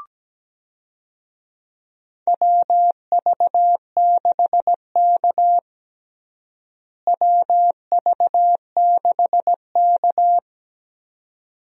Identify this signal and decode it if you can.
Morse code, heard as a keyed tone, about 17 wpm, 700 Hz. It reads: WV6K WV6K
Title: USA Call Signs - 4 Characters - Repeated 2x - 17wpm